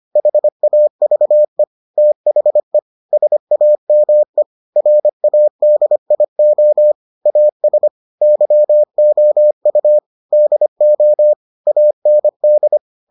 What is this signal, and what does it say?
Morse code, heard as a keyed tone, about 25 wpm, 600 Hz. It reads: HAVE THE SAME RADIO AS YOU DO AND